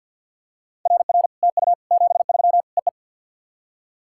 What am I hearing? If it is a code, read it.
Morse code, heard as a keyed tone, about 50 wpm, 700 Hz. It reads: RR TU 73 EE
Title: SOTA QSO Practice 50wpm